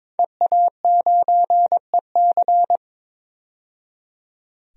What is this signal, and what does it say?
Morse code, heard as a keyed tone, about 22 wpm, 700 Hz. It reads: EA9EC
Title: DXCC-100 Call Signs - 5 Characters - 22wpm